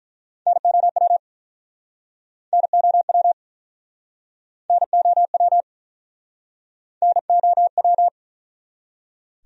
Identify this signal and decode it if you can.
Morse code, heard as a keyed tone, about 48 wpm, 700 Hz. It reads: NOW NOW NOW NOW